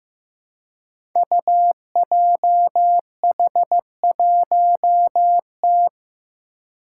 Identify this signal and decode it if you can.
Morse code, heard as a keyed tone, about 15 wpm, 700 Hz. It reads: UJH1T
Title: Five Challenging Letter-Number Combinations 15wpm